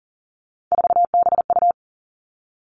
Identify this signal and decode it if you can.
Morse code, heard as a keyed tone, about 43 wpm, 700 Hz. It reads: 4BU
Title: Three Challenging Letter-Number Combinations 40wpm